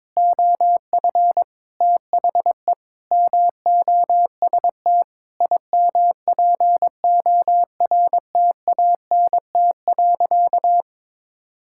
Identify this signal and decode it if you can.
Morse code, heard as a keyed tone, about 22 wpm, 700 Hz. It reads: OF THE MOST IMPORTANT.
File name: practice / 22wpm / Sentences with 8-Word Phrases - 22wpm